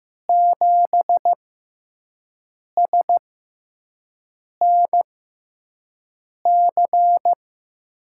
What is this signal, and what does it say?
Morse code, heard as a keyed tone, about 15 wpm, spaced as for 5 wpm, 700 Hz. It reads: 7SNC
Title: USA Call Signs - 6 Characters - 6x Character Spacing - 15wpm